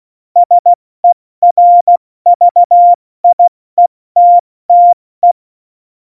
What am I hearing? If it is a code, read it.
Morse code, heard as a keyed tone, about 16 wpm, 700 Hz. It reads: SERVIETTE
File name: Morsekurs / Witze / Witz 09. April 2019, 70 F80